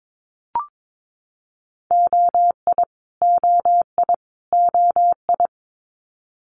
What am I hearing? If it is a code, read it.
Morse code, heard as a keyed tone, about 22 wpm, 700 Hz. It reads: OI OI OI
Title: Multi-letter Phonemes Repeated 3x 22wpm